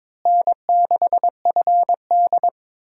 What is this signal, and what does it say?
Morse code, heard as a keyed tone, about 22 wpm, 700 Hz. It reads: N6FD
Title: USA Call Signs - 4 Characters - Repeated 2x - 22wpm